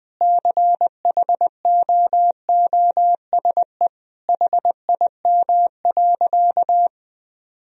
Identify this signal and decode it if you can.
Morse code, heard as a keyed tone, about 19 wpm, 700 Hz. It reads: CHOOSE HIM.
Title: Sentences from Top 1000 Words - Review Part 4 - 20wpm